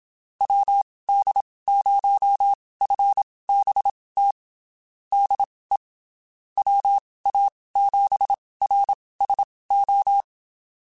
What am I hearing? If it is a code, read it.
Morse code, heard as a keyed tone, about 27 wpm, spaced as for 19 wpm, 800 Hz. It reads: WD0FBT DE WA7RSO
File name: QSOs Spaced